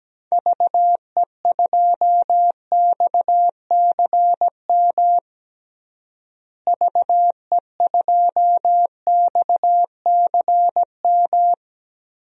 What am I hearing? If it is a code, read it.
Morse code, heard as a keyed tone, about 17 wpm, 700 Hz. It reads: VE2XCM VE2XCM
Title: DXCC-100 Call Signs - 6 Characters - Repeated 2x - 17wpm